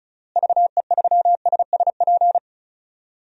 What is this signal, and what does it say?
Morse code, heard as a keyed tone, about 34 wpm, 700 Hz. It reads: VE3SSP
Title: DXCC-100 Call Signs - 6 Characters - Repeated 3x - 35wpm